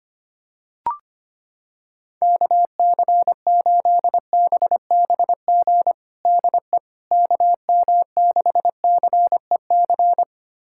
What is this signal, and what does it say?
Morse code, heard as a keyed tone, about 25 wpm, 700 Hz. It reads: KC8BBG DE KM6CEC
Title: USA Callsign DE Callsign 25wpm